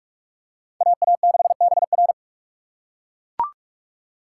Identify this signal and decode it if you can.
Morse code, heard as a keyed tone, about 41 wpm, 700 Hz. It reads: AA6BR